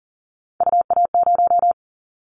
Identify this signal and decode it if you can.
Morse code, heard as a keyed tone, about 42 wpm, 700 Hz. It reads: UA0
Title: Callsign Prefixes for Top 50 DXCC - 40wpm